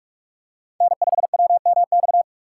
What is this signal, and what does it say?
Morse code, heard as a keyed tone, about 42 wpm, 700 Hz. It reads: N5WMX